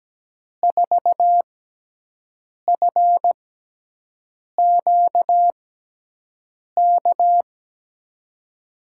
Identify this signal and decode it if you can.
Morse code, heard as a keyed tone, about 17 wpm, spaced as for 6 wpm, 700 Hz. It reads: 4FQK
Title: USA Call Signs - 6x Character Spacing - 17wpm